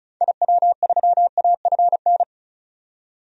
Morse code, heard as a keyed tone, about 36 wpm, 700 Hz. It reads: IW3AFN
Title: DXCC-100 Call Signs - 6 Characters - 35wpm